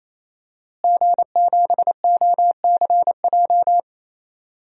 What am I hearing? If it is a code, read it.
Morse code, heard as a keyed tone, about 28 wpm, 700 Hz. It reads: G7OCJ
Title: DXCC-100 Call Signs - 5 Characters - Repeated 3x - 28wpm